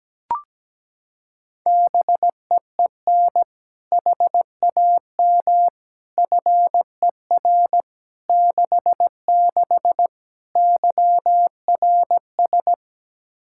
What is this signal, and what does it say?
Morse code, heard as a keyed tone, about 17 wpm, 700 Hz. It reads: BEEN HAM FER 66 YRS
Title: Practice QSO - Exchange 3 - More Info - 17wpm